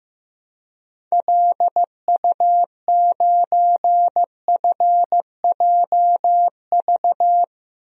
Morse code, heard as a keyed tone, about 15 wpm, 700 Hz. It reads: LU9FJV
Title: DXCC-100 Call Signs - 6 Characters - Repeated 3x - 15wpm